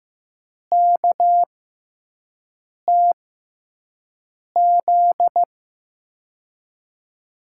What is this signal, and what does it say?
Morse code, heard as a keyed tone, about 15 wpm, spaced as for 5 wpm, 700 Hz. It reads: KTZ
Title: USA Call Signs - 6x Character Spacing - 15wpm